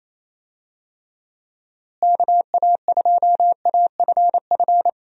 Morse code, heard as a keyed tone, about 28 wpm, 700 Hz. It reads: KA2AFF
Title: USA Call Signs - 6 Characters - Repeated 3x - 28wpm